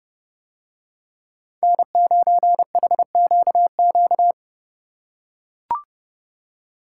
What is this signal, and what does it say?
Morse code, heard as a keyed tone, about 27 wpm, 700 Hz. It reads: N9HQQ